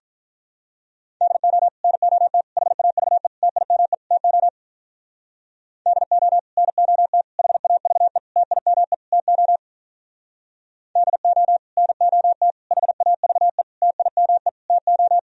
Morse code, heard as a keyed tone, about 45 wpm, 700 Hz. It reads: DO NOT HAVE TIME TO DO NOT HAVE TIME TO DO NOT HAVE TIME TO